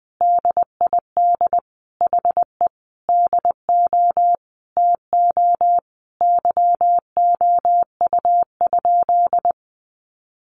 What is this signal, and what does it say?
Morse code, heard as a keyed tone, about 20 wpm, 700 Hz. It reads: DID HE DO TO YOU?
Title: Sentences from Top 100 Words 20wpm